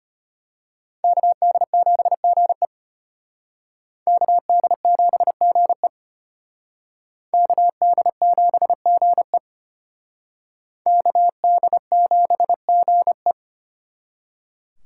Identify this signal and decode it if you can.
Morse code, heard as a keyed tone, about 32 wpm, 700 Hz. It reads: KD7GE KD7GE KD7GE KD7GE